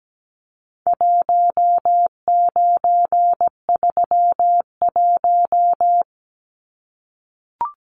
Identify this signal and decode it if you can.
Morse code, heard as a keyed tone, about 17 wpm, 700 Hz. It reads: 1931
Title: Numbers - 4-Digit 17wpm